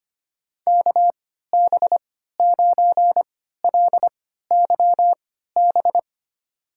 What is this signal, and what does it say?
Morse code, heard as a keyed tone, about 25 wpm, spaced as for 14 wpm, 700 Hz. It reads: KB9LYB
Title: USA Call Signs - 6 Characters - 3x Character Spacing - 25wpm